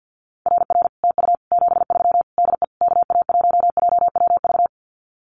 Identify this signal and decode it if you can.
Morse code, heard as a keyed tone, about 50 wpm, 700 Hz. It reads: RR TU 73 DE KA1JWV
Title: Common QSO Phrases Encore 50wpm